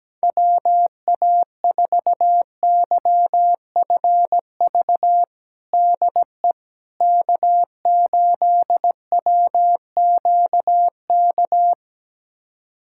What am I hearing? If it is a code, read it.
Morse code, heard as a keyed tone, about 17 wpm, 700 Hz. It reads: WA4YFV DE K8WQK
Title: USA Callsign DE Callsign 17wpm